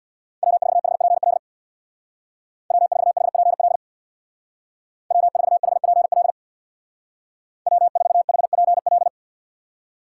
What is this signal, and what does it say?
Morse code, heard as a keyed tone, about 60 wpm, 700 Hz. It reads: W4HPL W4HPL W4HPL W4HPL
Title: Speed Racing - USA 1x3 Call Signs 50wpm